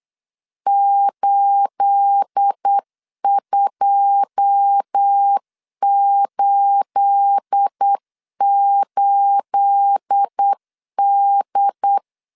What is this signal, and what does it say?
Morse code, heard as a keyed tone, about 8 wpm, 800 Hz. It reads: 8288D
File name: LEZIONI / 9 28 8wpm 3512